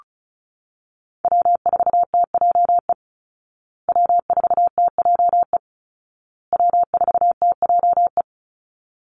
Morse code, heard as a keyed tone, about 34 wpm, 700 Hz. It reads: W4TJE W4TJE W4TJE
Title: Call Signs Repeated 2x - 35wpm